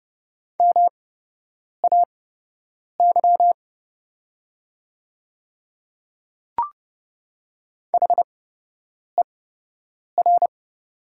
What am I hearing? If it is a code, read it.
Morse code, heard as a keyed tone, about 30 wpm, spaced as for 8 wpm, 700 Hz. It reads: MAY HER
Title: 8x Character Spacing - Top 100 Words - 30wpm